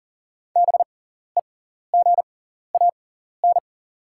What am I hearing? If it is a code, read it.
Morse code, heard as a keyed tone, about 40 wpm, spaced as for 14 wpm, 690 Hz. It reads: BEGAN
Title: 6x Character Spacing - Top 500 Words - 40wpm